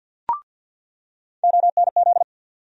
Morse code, heard as a keyed tone, about 53 wpm, 700 Hz. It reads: ONZ
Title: USA Call Sign Postfixes - 3 Letters 50wpm